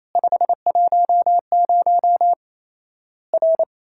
Morse code, heard as a keyed tone, about 28 wpm, 700 Hz. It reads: 510 R
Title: Worked All Europe DX Contest - QTC Exchange - 28wpm